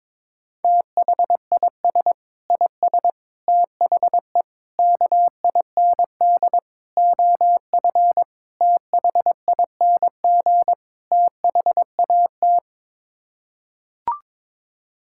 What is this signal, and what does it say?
Morse code, heard as a keyed tone, about 22 wpm, 700 Hz. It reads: THIS IS THE KIND OF THING THAT
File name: practice / 22wpm / Sets of 7 Words - 22wpm